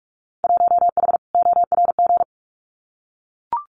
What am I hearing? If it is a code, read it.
Morse code, heard as a keyed tone, about 44 wpm, 700 Hz. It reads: 1H ORG
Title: Winter Field Day Exchanges 45wpm